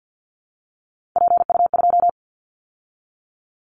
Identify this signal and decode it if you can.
Morse code, heard as a keyed tone, about 50 wpm, 700 Hz. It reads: PU2
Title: Callsign Prefixes for Top 50 DXCC - 50wpm